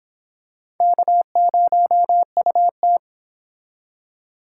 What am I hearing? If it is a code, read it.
Morse code, heard as a keyed tone, about 26 wpm, 700 Hz. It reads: K0UT